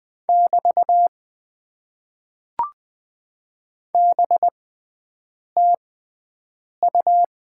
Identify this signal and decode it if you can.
Morse code, heard as a keyed tone, about 20 wpm, spaced as for 7 wpm, 700 Hz. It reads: <BT> BTU
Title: Top 90 QSO Elements - 6x Character Spacing - 20wpm